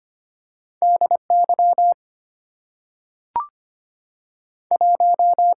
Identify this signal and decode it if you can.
Morse code, heard as a keyed tone, about 25 wpm, 700 Hz. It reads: DY 1
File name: Two Letter-Number Combinations - 25wpm